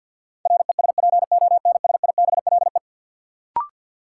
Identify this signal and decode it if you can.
Morse code, heard as a keyed tone, about 53 wpm, 700 Hz. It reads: RESPONSIBLE
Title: Sentences from Top 1600 Words - New Word Drill Alt - 50wpm